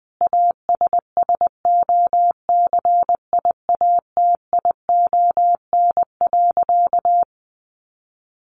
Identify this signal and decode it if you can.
Morse code, heard as a keyed tone, about 20 wpm, 700 Hz. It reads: ASSOCIATION.
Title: Sentences from Top 1200 Words 20wpm